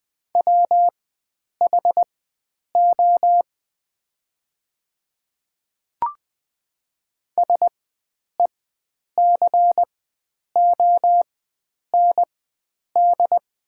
Morse code, heard as a keyed tone, about 20 wpm, spaced as for 9 wpm, 700 Hz. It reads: WHO SECOND
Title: 4x Character Spacing - Top 300 Words - 20wpm